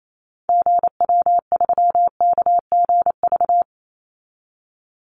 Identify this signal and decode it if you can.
Morse code, heard as a keyed tone, about 28 wpm, 700 Hz. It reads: GW3KGV
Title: Call Signs Repeated 3x - 28wpm